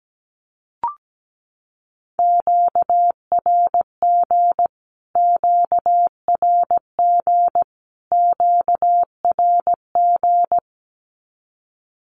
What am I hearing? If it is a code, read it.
Morse code, heard as a keyed tone, about 17 wpm, 700 Hz. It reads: QRG QRG QRG
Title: Q-Codes Repeated 3x 17wpm